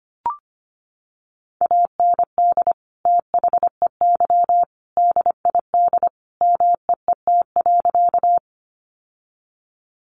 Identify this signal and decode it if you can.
Morse code, heard as a keyed tone, about 25 wpm, 690 Hz. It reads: AND THEY DID MEET.
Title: Sentences from Top 1100 Words 25wpm